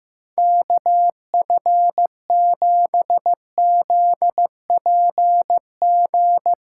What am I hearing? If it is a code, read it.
Morse code, heard as a keyed tone, about 16 wpm, 700 Hz. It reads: KF7ZPG
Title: Speed Racing - USA 2x3 Call Signs 15wpm